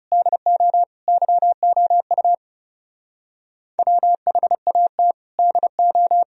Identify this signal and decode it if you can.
Morse code, heard as a keyed tone, about 39 wpm, 700 Hz. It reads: DO YOU WHAT DO